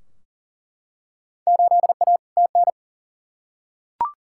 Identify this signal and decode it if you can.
Morse code, heard as a keyed tone, about 40 wpm, 700 Hz. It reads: 8A TN
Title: Field Day Exchanges Encore - 40wpm